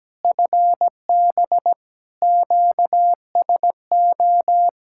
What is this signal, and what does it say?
Morse code, heard as a keyed tone, about 17 wpm, 700 Hz. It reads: FB QSO
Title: Practice QSO - Exchange 4 - Ending - 17wpm